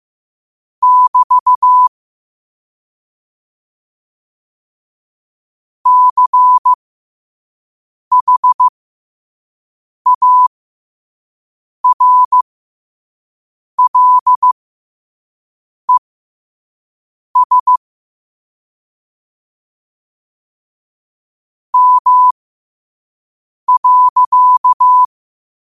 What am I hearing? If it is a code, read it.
Morse code, heard as a keyed tone, about 15 wpm, spaced as for 5 wpm, 1000 Hz. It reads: <BT> CHARLES M.
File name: qotdfiles / 5wpm